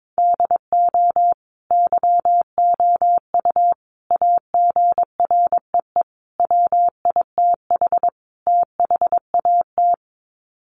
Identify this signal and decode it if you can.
Morse code, heard as a keyed tone, about 22 wpm, 700 Hz. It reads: DO YOU AGREE WITH THAT
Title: Sets of 5 Words Encore 22wpm